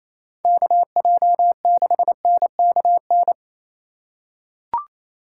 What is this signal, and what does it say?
Morse code, heard as a keyed tone, about 28 wpm, 700 Hz. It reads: KJ6NKN
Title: USA Call Signs - 2x Character Spacing - 28wpm